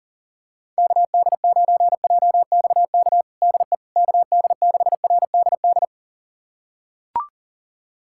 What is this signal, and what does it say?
Morse code, heard as a keyed tone, about 40 wpm, 700 Hz. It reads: KD9JXK DE KD6RDD
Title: USA Callsign DE Callsign - 6 Characters - 40wpm